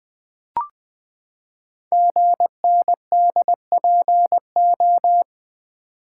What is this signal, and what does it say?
Morse code, heard as a keyed tone, about 20 wpm, 700 Hz. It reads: GNDPO
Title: Code Groups - 5 Letters - 20wpm